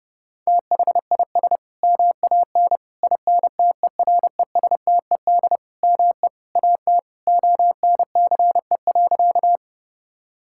Morse code, heard as a keyed tone, about 30 wpm, 700 Hz. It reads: THIS MAN INTERESTED ME AT ONCE.
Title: Sentences from Top 1000 Words - Review Part 6 - 30wpm